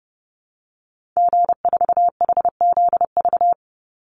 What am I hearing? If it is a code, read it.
Morse code, heard as a keyed tone, about 30 wpm, 700 Hz. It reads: G4HZV
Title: Call Signs Repeated 3x - 30wpm